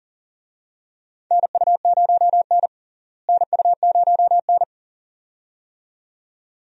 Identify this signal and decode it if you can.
Morse code, heard as a keyed tone, about 40 wpm, 700 Hz. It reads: NU0N NU0N
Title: USA Call Signs - 4 Characters - Repeated 2x - 40wpm